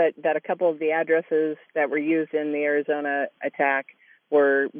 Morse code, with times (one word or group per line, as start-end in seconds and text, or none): none